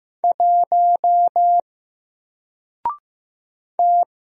Morse code, heard as a keyed tone, about 15 wpm, 700 Hz. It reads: 1 T
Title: Single Character (Letter:Number:Punctuation:Prosign) Alt 15wpm